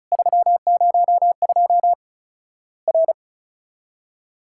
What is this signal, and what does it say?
Morse code, heard as a keyed tone, about 34 wpm, 700 Hz. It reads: 302 R
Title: Worked All Europe DX Contest - QTC Exchange - 35wpm